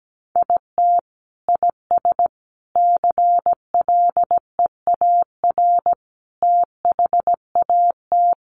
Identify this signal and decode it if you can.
Morse code, heard as a keyed tone, about 17 wpm, 700 Hz. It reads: IT IS CLEAR THAT
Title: Sets of 5 Words Encore 17wpm